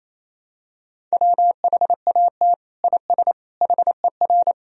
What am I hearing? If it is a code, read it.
Morse code, heard as a keyed tone, about 28 wpm, 700 Hz. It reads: WHAT IS HER
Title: Sentences from Top 1000 Words - Review Part 1 - 28wpm